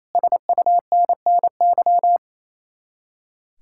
Morse code, heard as a keyed tone, about 35 wpm, 700 Hz. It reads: SUNNY
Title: Speed-Racing Common QSO Phrases - 28wpm